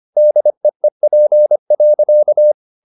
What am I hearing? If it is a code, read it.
Morse code, heard as a keyed tone, about 25 wpm, 600 Hz. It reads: DEEP.
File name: the old man and the sea part 1 25wpm